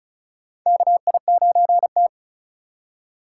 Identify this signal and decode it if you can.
Morse code, heard as a keyed tone, about 36 wpm, 700 Hz. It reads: KI9T
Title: USA Call Signs - 4 Characters - 2x Character Spacing - 35wpm